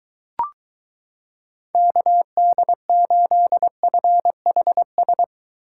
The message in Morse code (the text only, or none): KD8FHS